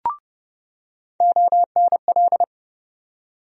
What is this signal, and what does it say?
Morse code, heard as a keyed tone, about 31 wpm, 700 Hz. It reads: ONL
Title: USA Call Sign Postfixes - 3 Letters 30wpm